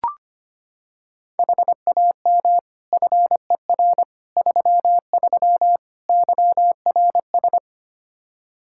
Morse code, heard as a keyed tone, about 25 wpm, 700 Hz. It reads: HAM FER 33 YRS
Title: Practice QSO - Exchange 3 - More Info - 25wpm